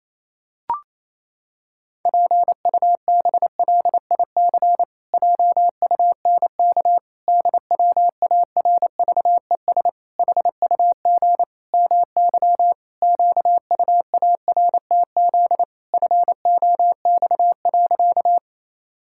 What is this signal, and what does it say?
Morse code, heard as a keyed tone, about 28 wpm, 700 Hz. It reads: PUBLIC JUNK DWARVES HUG MY QUARTZ FOX.
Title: Pangram Challenge v2 - 28wpm